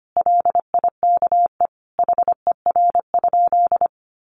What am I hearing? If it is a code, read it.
Morse code, heard as a keyed tone, about 25 wpm, 700 Hz. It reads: LIKE HER?
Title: Sentences from Top 100 Words 25wpm